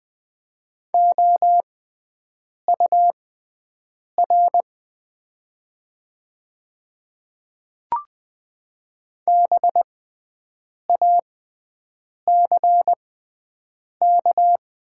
Morse code, heard as a keyed tone, about 20 wpm, spaced as for 7 wpm, 690 Hz. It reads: OUR BACK
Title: 6x Character Spacing - Top 200 Words - 20wpm